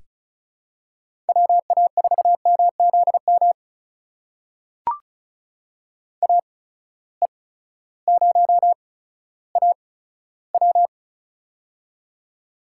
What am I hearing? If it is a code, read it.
Morse code, heard as a keyed tone, about 36 wpm, spaced as for 16 wpm, 700 Hz. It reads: WA4MZM AE0AW
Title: USA Call Signs - 8x Character Spacing - 35wpm